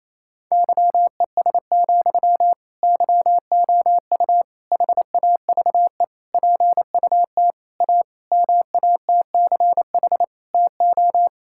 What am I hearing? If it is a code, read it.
Morse code, heard as a keyed tone, about 27 wpm, 700 Hz. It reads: YES, YOU HAVE PUT A MATCH TO
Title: Sentences from Top 1600 Words 28wpm